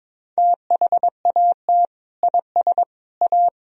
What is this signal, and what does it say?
Morse code, heard as a keyed tone, about 22 wpm, 700 Hz. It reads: THAT IS A
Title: Sentences from Top 1000 Words - Review Part 3 - 22wpm